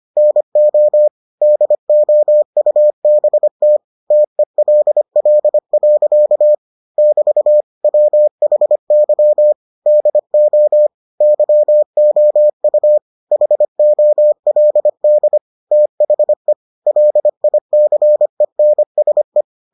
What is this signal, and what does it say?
Morse code, heard as a keyed tone, about 25 wpm, 600 Hz. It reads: NO DOUBT TELL. <BT> WHY DO YOU HOLD THE LICENSE